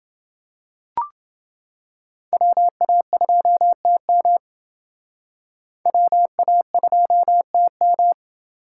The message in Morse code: WA2TM WA2TM

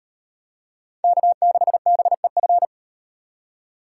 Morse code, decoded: K6BEF

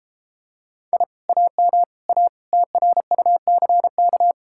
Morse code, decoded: I AM A TRUCK

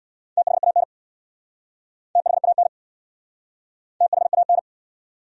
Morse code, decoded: THAN THAN THAN

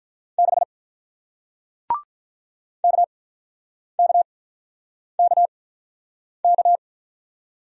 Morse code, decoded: 6 K K K K

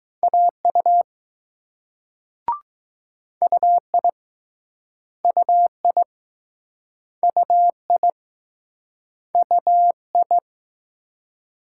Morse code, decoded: AU UI UI UI UI

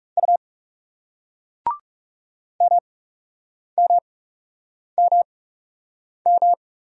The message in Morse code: U MMMM

